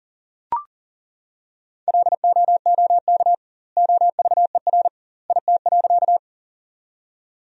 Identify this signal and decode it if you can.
Morse code, heard as a keyed tone, about 40 wpm, 700 Hz. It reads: LOOK OVER IT.
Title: 2x Word-Spacing - Sentences from Top 100 Words - 40wpm